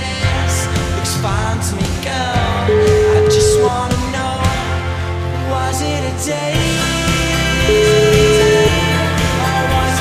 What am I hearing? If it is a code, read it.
Ringback tone (Europe, 425 Hz), with noise.